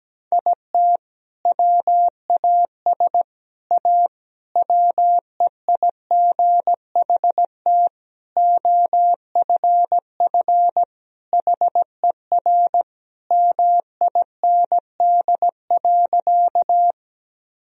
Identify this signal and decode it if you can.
Morse code, heard as a keyed tone, about 17 wpm, 700 Hz. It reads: IT WAS A WEIGHT OFF HER MIND.
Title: Sentences from Top 1000 Words - Review Part 3 - 17wpm